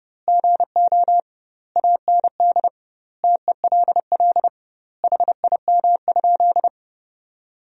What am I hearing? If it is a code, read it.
Morse code, heard as a keyed tone, about 32 wpm, 700 Hz. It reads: GO AND TELL HIM?
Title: 2x Word-Spacing - Sentences from Top 200 Words - 30wpm